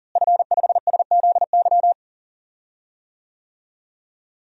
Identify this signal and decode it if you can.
Morse code, heard as a keyed tone, about 40 wpm, 700 Hz. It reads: F5SZY
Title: DXCC-100 Call Signs - 5 Characters - Repeated 3x - 40wpm